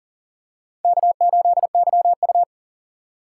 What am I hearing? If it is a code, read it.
Morse code, heard as a keyed tone, about 37 wpm, 700 Hz. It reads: K8YU
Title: USA Call Signs - 4 Characters - 3x Character Spacing - 40wpm